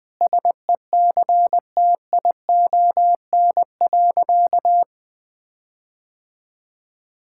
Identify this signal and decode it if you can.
Morse code, heard as a keyed tone, about 21 wpm, 700 Hz. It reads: SECTION.
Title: Sentences with 9-Word Phrases - 20wpm